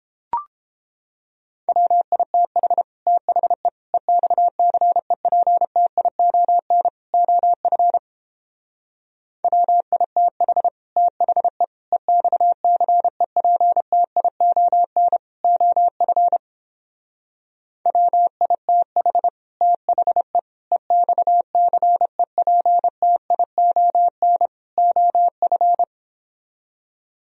Morse code, heard as a keyed tone, about 29 wpm, 700 Hz. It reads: WITH THE EXCEPTION OF WITH THE EXCEPTION OF WITH THE EXCEPTION OF